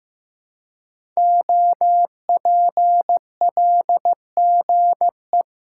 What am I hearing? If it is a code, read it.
Morse code, heard as a keyed tone, about 15 wpm, 700 Hz. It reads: OPLGE
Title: Code Groups - 5 Letters - 15wpm